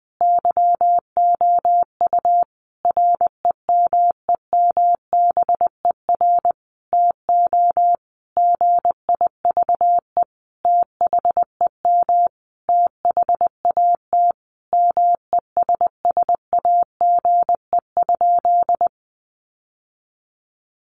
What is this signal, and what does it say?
Morse code, heard as a keyed tone, about 20 wpm, 700 Hz. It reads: YOU REMEMBER TO GIVE THEM THAT MESSAGE?
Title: Sentences from Top 1100 Words 20wpm